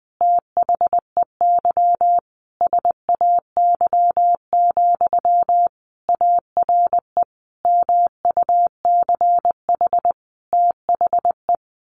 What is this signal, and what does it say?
Morse code, heard as a keyed tone, about 20 wpm, 700 Hz. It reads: THEY SAY, ARE MUCH THE
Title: Sentences from Top 1100 Words 20wpm